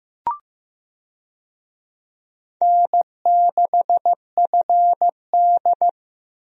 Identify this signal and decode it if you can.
Morse code, heard as a keyed tone, about 15 wpm, 700 Hz. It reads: N6FD
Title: USA Call Signs - 4 Characters - Repeated 2x - 15wpm